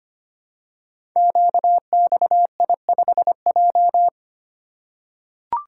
Morse code, heard as a keyed tone, about 25 wpm, 700 Hz. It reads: QXI5J
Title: Code Groups - 5 Letter:Numbers - 25wpm